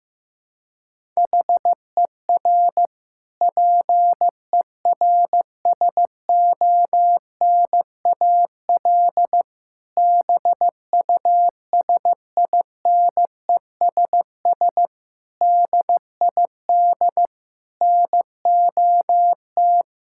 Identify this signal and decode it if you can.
Morse code, heard as a keyed tone, about 15 wpm, 700 Hz. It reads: HER PERSONAL BUSINESS DID NOT
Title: Sentences from Top 1000 Words - Review Part 6 - 15wpm